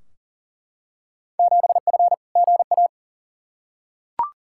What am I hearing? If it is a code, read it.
Morse code, heard as a keyed tone, about 40 wpm, 700 Hz. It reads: 7F GA
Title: Field Day Exchanges Encore - 40wpm